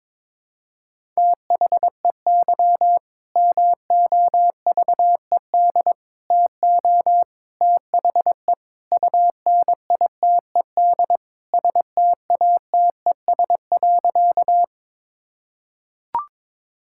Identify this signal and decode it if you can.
Morse code, heard as a keyed tone, about 22 wpm, 700 Hz. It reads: THEY MOVED TO THE UNITED STATES.
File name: practice / 22wpm / Sentences with 4-Word Phrases - 22wpm